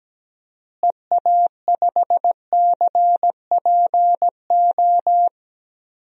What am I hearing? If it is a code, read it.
Morse code, heard as a keyed tone, about 17 wpm, 700 Hz. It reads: EA5CPO
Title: DXCC-100 Call Signs - 6 Characters - Repeated 2x - 17wpm